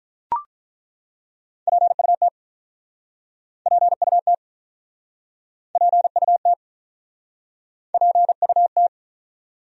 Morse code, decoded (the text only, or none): PUT PUT PUT PUT